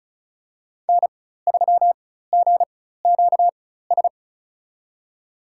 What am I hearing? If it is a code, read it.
Morse code, heard as a keyed tone, about 36 wpm, spaced as for 16 wpm, 700 Hz. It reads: N3GQS
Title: USA Call Signs - 5 Characters - 4x Character Spacing - 35wpm